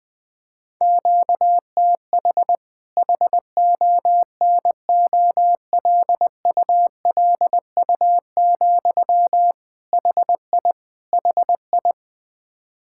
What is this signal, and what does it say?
Morse code, heard as a keyed tone, about 20 wpm, 700 Hz. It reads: QTH HONOLULU, HI HI
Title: Practice QSO - Exchange 1 - Beginning - 20wpm